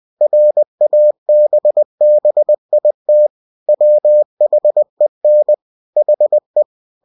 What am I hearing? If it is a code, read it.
Morse code, heard as a keyed tone, about 20 wpm, 600 Hz. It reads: RABBIT WHEN HE
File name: cw-books / winnie-the-pooh / winnie-20wpm-page-0063